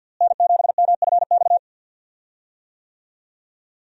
Morse code, heard as a keyed tone, about 51 wpm, 700 Hz. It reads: N7MFX
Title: USA Call Signs - 5 Characters - Repeated 2x - 50wpm